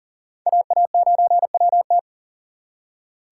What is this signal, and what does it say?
Morse code, heard as a keyed tone, about 54 wpm, 700 Hz. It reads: AA9WT